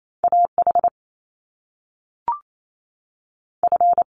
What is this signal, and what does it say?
Morse code, heard as a keyed tone, about 28 wpm, 700 Hz. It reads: A H F
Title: Top Two Letter Combinations in Words 28wpm